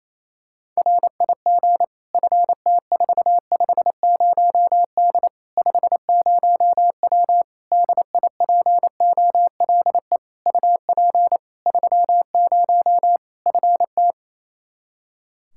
Morse code, heard as a keyed tone, about 28 wpm, 700 Hz. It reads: RIG FT450D 50W DIPOLE UP 30 FT
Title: Practice QSO - Exchange 2 - Info - 28wpm